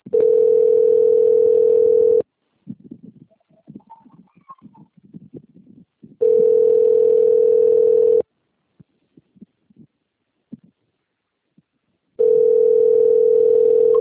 Ringback tone (North America), 440 + 480 Hz.